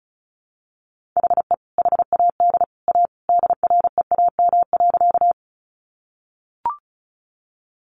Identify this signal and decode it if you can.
Morse code, heard as a keyed tone, about 34 wpm, 700 Hz. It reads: HE HAD A DREAM.
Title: Sentences from Top 1500 Words 35wpm